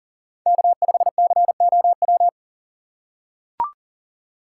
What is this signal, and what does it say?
Morse code, heard as a keyed tone, about 39 wpm, 700 Hz. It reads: K5COW
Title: USA Call Signs - 5 Characters - 6x Character Spacing - 40wpm